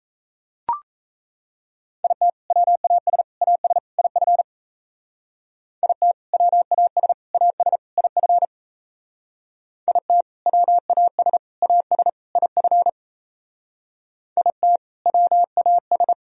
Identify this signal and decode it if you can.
Morse code, heard as a keyed tone, about 34 wpm, 700 Hz. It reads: IT WAS AS IF IT WAS AS IF IT WAS AS IF IT WAS
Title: Speed-Racing - Sets of 4 Words 28wpm